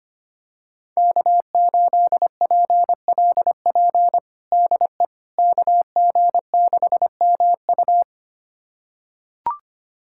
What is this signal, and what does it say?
Morse code, heard as a keyed tone, about 25 wpm, 700 Hz. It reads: K8PLP DE KG6MU
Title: USA Callsign DE Callsign - 5 Characters - 25wpm